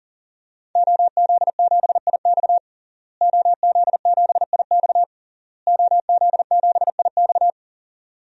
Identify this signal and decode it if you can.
Morse code, heard as a keyed tone, about 40 wpm, 700 Hz. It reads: OZ7IX OZ7IX OZ7IX
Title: DXCC-100 Call Signs - 5 Characters - Repeated 3x - 40wpm